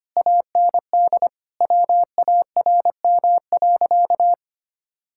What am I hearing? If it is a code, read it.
Morse code, heard as a keyed tone, about 25 wpm, 700 Hz. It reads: AND WARM.
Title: Sentences from Top 1600 Words 25wpm